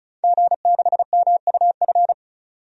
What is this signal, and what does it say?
Morse code, heard as a keyed tone, about 34 wpm, 700 Hz. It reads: G6MUF
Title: DXCC-100 Call Signs - 5 Characters - 35wpm